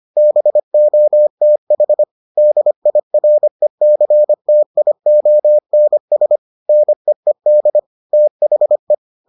Morse code, heard as a keyed tone, about 25 wpm, 600 Hz. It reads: BOTH DIRECTIONS NEED THE